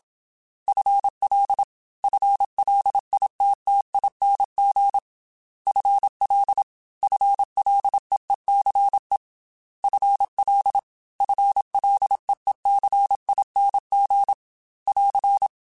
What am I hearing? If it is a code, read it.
Morse code, heard as a keyed tone, about 27 wpm, 800 Hz. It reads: FL FLITTING FL FLEECE FL FLEECING <AR>
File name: FL / FL mp3s / FL Prefixes 1x1 @25